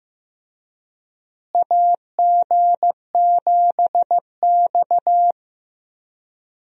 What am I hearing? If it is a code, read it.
Morse code, heard as a keyed tone, about 15 wpm, 700 Hz. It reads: AG7X
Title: USA Call Signs - 4 Characters - Repeated 3x - 15wpm